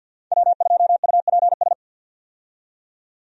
Morse code, heard as a keyed tone, about 48 wpm, 700 Hz. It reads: W2UPS
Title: USA Call Signs - 5 Characters - 50wpm